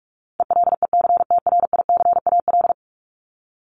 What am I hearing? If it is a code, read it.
Morse code, heard as a keyed tone, about 44 wpm, 700 Hz. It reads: ELECTRICAL
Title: Top 751-1000 Words 45wpm